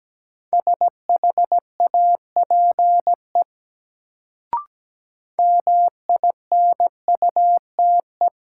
Morse code, heard as a keyed tone, about 17 wpm, 700 Hz. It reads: SHAPE MINUTE
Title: Sentences from Top 500 Words - New Word Drill - 17wpm